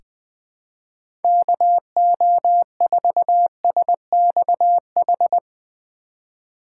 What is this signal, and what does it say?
Morse code, heard as a keyed tone, about 20 wpm, 700 Hz. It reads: KO4SXH
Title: 2023 POTA Hunter Call Signs - 20wpm